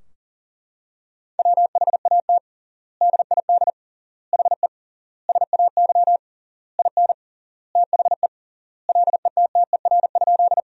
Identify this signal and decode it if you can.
Morse code, heard as a keyed tone, about 40 wpm, 700 Hz. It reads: WHAT DID HE SAY IN THE LETTER?
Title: 3x Word-Spacing - Sentences from Top 300 Words - 40wpm